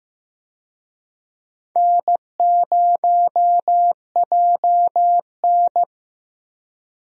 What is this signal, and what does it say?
Morse code, heard as a keyed tone, about 15 wpm, 700 Hz. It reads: N0JN
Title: USA Call Signs - 4 Characters - Repeated 2x - 15wpm